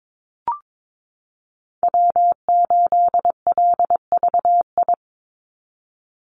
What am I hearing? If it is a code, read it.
Morse code, heard as a keyed tone, about 22 wpm, 700 Hz. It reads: W8LVI
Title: Call Signs Repeated 2x - 22wpm